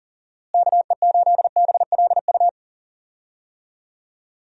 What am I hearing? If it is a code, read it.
Morse code, heard as a keyed tone, about 37 wpm, 700 Hz. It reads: KE8BLU
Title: USA Call Signs - 6 Characters - Repeated 3x - 40wpm